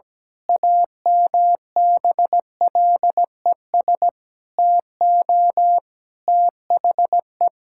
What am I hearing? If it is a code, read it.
Morse code, heard as a keyed tone, about 17 wpm, 700 Hz. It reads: AMBLES TO THE
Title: Pangram Challenge v2 - 17wpm